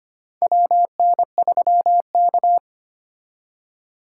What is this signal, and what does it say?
Morse code, heard as a keyed tone, about 25 wpm, 700 Hz. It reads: WN3K
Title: K1USN SST QSO Practice - 25wpm